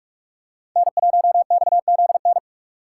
Morse code, heard as a keyed tone, about 44 wpm, 700 Hz. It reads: N1XZN